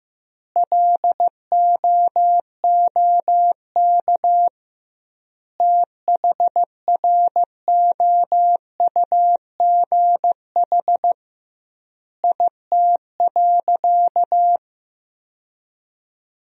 Morse code, heard as a keyed tone, about 15 wpm, 700 Hz. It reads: LOOK THROUGH IT.
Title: 2x Word-Spacing - Sentences from Top 200 Words - 15wpm